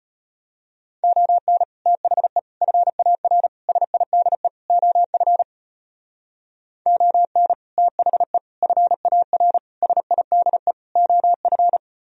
Morse code, heard as a keyed tone, about 33 wpm, 700 Hz. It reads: ON THE FAR SIDE OF ON THE FAR SIDE OF